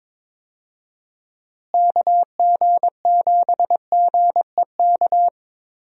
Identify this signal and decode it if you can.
Morse code, heard as a keyed tone, about 22 wpm, 700 Hz. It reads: KG7GEK
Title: USA Call Signs - 6 Characters - Repeated 2x - 22wpm